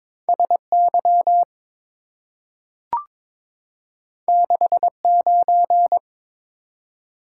Morse code, heard as a keyed tone, about 23 wpm, 700 Hz. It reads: SY 69
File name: practice / 22wpm / Two Letter-Number Combinations - 22wpm